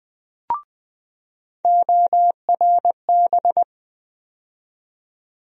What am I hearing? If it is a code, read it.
Morse code, heard as a keyed tone, about 20 wpm, 700 Hz. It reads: ORB
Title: USA Call Sign Postfixes - 3 Letters 20wpm